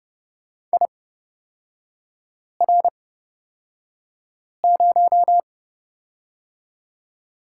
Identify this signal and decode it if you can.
Morse code, heard as a keyed tone, about 30 wpm, spaced as for 5 wpm, 690 Hz. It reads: IR0
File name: Lesson 38e - Letter q - Three Characters 30wpm